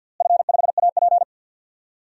Morse code, heard as a keyed tone, about 45 wpm, 700 Hz. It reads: F5AP
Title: DXCC-100 Call Signs - 4 Characters - Repeated 3x - 50wpm